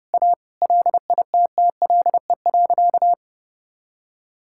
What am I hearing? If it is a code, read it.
Morse code, heard as a keyed tone, about 30 wpm, 700 Hz. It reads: A LITTLE.
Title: Sentences from Top 1000 Words - Review Part 5 - 30wpm